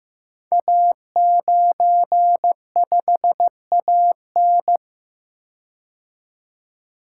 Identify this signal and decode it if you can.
Morse code, heard as a keyed tone, about 15 wpm, 700 Hz. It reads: A95AN
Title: Code Groups - 5 Letter:Numbers - 15wpm